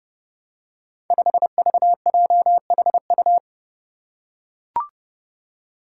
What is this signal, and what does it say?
Morse code, heard as a keyed tone, about 30 wpm, 700 Hz. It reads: 5VJHU
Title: Five Challenging Letter-Number Combinations 30wpm